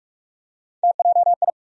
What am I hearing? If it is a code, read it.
Morse code, heard as a keyed tone, about 48 wpm, 700 Hz. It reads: TJI
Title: USA Call Sign Postfixes - 3 Letters 45wpm